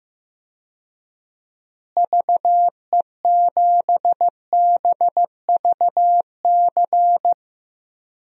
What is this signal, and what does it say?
Morse code, heard as a keyed tone, about 15 wpm, 700 Hz. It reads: VE7BVC
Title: DXCC-100 Call Signs - 6 Characters - Repeated 2x - 15wpm